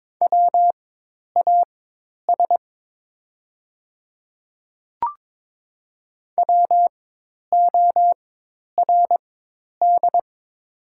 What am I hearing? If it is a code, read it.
Morse code, heard as a keyed tone, about 22 wpm, spaced as for 10 wpm, 700 Hz. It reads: WAS WORD
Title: 4x Character Spacing - Top 100 Words - 22wpm